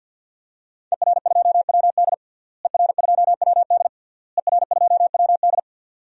Morse code, heard as a keyed tone, about 50 wpm, 700 Hz. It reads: ER2WD ER2WD ER2WD
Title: DXCC-100 Call Signs - 5 Characters - Repeated 3x - 50wpm